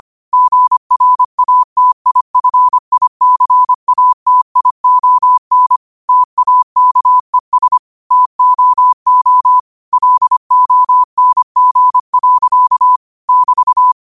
Morse code, heard as a keyed tone, about 25 wpm, 1000 Hz. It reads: GRATIFICATION TAKES TOO LONG. <BT>
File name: qotdfiles / 25wpm